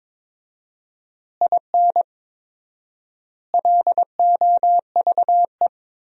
Morse code, heard as a keyed tone, about 22 wpm, 700 Hz. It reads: IN LOVE
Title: 4x Word-Spacing - Sentences from Top 400 Words - 22wpm